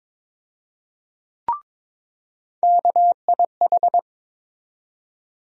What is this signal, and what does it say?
Morse code, heard as a keyed tone, about 22 wpm, 700 Hz. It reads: KIH
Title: USA Call Sign Postfixes - 3 Letters 22wpm